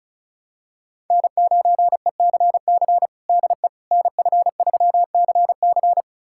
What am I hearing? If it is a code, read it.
Morse code, heard as a keyed tone, about 36 wpm, 700 Hz. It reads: N9ECC DE NF3CC
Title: USA Callsign DE Callsign 35wpm